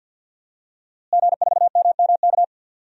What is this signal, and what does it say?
Morse code, heard as a keyed tone, about 50 wpm, 700 Hz. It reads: G4MMK
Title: DXCC-100 Call Signs - 5 Characters - Repeated 3x - 50wpm